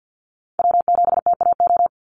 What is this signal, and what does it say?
Morse code, heard as a keyed tone, about 50 wpm, 700 Hz. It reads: W7TAO
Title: NA SOTA Chaser Callsigns 50wpm